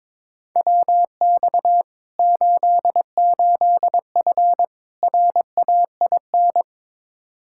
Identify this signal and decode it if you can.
Morse code, heard as a keyed tone, about 22 wpm, 700 Hz. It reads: WX 88F RAIN
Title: Practice QSO - Exchange 2 - Info - 22wpm